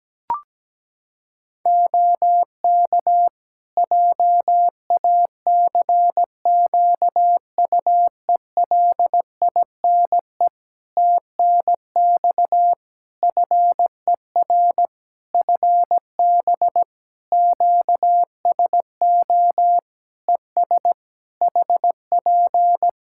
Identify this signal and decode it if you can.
Morse code, heard as a keyed tone, about 17 wpm, 700 Hz. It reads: OK JACQUELINE TNX FER FB QSO ES HP
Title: K1USN SST QSO Practice 17wpm